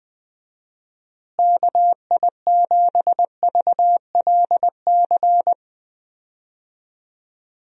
Morse code, heard as a keyed tone, about 21 wpm, 700 Hz. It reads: KI7VLC
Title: USA Call Signs - 6 Characters - Repeated 3x - 20wpm